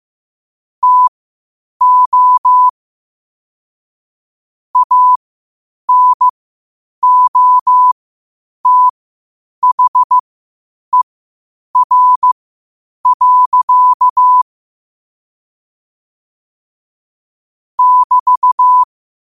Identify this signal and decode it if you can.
Morse code, heard as a keyed tone, about 15 wpm, spaced as for 8 wpm, 1000 Hz. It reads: TO ANOTHER. <BT>
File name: qotdfiles / 8wpm